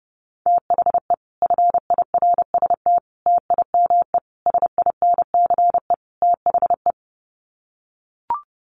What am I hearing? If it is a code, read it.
Morse code, heard as a keyed tone, about 30 wpm, 700 Hz. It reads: THE FIRST TIME SINCE THE
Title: Sets of 5 Words Encore 30wpm